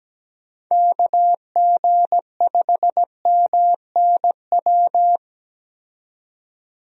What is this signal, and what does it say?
Morse code, heard as a keyed tone, about 17 wpm, 700 Hz. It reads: KG5MNW
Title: USA Call Signs - 6 Characters - 17wpm